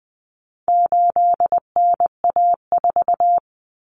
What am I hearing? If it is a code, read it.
Morse code, heard as a keyed tone, about 20 wpm, 700 Hz. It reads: 8NA4
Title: Common Cut Numbers - 4-Digit 20wpm